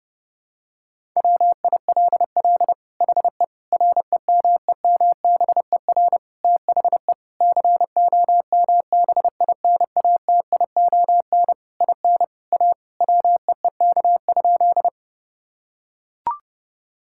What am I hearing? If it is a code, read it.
Morse code, heard as a keyed tone, about 32 wpm, 700 Hz. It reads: WILL HE REMEMBER THE COMBINATION IN A WEEK?
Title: Sentences from Top 1000 Words - Review Part 6 - 30wpm